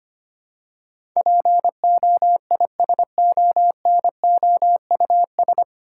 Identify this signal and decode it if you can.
Morse code, heard as a keyed tone, about 25 wpm, 700 Hz. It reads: POISONOUS
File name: Nine Letter Words - 25wpm